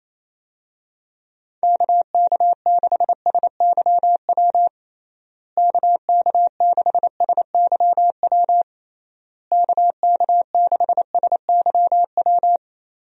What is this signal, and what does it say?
Morse code, heard as a keyed tone, about 27 wpm, 700 Hz. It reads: KK6SYW KK6SYW KK6SYW
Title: USA Call Signs - 6 Characters - Repeated 3x - 28wpm